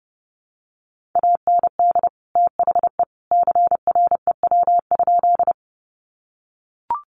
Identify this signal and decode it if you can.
Morse code, heard as a keyed tone, about 30 wpm, 700 Hz. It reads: AND THE CREW?
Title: Sentences from Top 1500 Words 30wpm